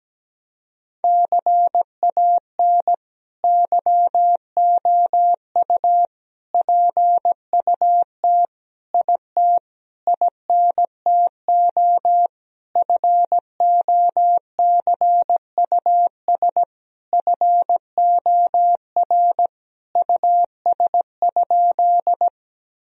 Morse code, heard as a keyed tone, about 17 wpm, 700 Hz. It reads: CAN YOU PUT IT INTO FOCUS FOR US?
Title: Sentences from Top 1000 Words - Review Part 4 - 17wpm